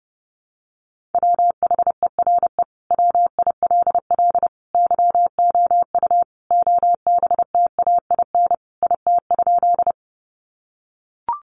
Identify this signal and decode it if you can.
Morse code, heard as a keyed tone, about 30 wpm, 700 Hz. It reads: WHERE WILL YOU OBTAIN IT?
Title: Sentences from Top 1300 Words 30wpm